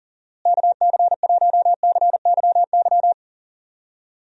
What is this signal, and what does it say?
Morse code, heard as a keyed tone, about 40 wpm, 700 Hz. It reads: KC1CYY
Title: USA Call Signs - 6 Characters - Repeated 2x - 40wpm